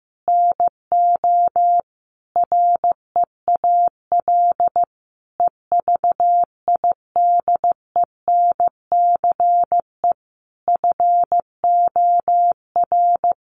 Morse code, heard as a keyed tone, about 14 wpm, 700 Hz. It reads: NO REAL EVIDENCE FOR
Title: Sentences from Top 1300 Words 15wpm